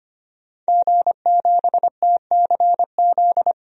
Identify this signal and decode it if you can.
Morse code, heard as a keyed tone, about 25 wpm, 700 Hz. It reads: G7TCZ
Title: DXCC-100 Call Signs - 5 Characters - 25wpm